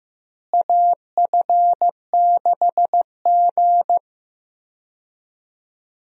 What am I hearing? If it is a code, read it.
Morse code, heard as a keyed tone, about 15 wpm, 700 Hz. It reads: AF6G
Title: USA Call Signs - 4 Characters - Repeated 3x - 15wpm